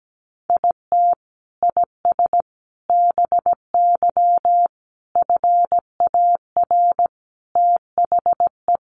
Morse code, heard as a keyed tone, about 17 wpm, 700 Hz. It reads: IT IS BY FAR THE
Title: Sentences from Top 1200 Words 17wpm